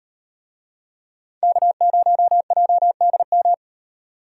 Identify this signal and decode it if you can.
Morse code, heard as a keyed tone, about 38 wpm, 700 Hz. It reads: K0JDM